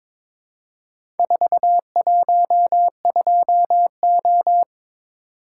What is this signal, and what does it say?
Morse code, heard as a keyed tone, about 23 wpm, 700 Hz. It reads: 412O
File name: practice / 22wpm / Four Letter-Number Combinations - 22wpm